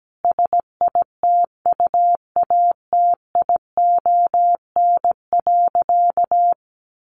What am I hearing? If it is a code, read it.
Morse code, heard as a keyed tone, about 17 wpm, 700 Hz. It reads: SITUATION.
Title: Sentences from Top 1200 Words 17wpm